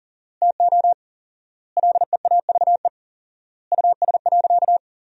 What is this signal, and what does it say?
Morse code, heard as a keyed tone, about 37 wpm, 700 Hz. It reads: TO LEAVE US.
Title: 4x Word-Spacing - Sentences from Top 400 Words - 40wpm